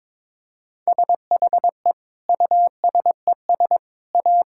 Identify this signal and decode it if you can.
Morse code, heard as a keyed tone, about 22 wpm, 700 Hz. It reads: SHE USES A
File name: Difficult Sentences with Hints - 22wpm